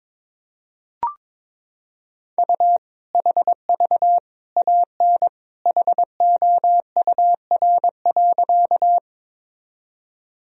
Morse code, heard as a keyed tone, about 21 wpm, 700 Hz. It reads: U HV AN HOUR.
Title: Sentences from Top 400 Words and Abbreviations 22wpm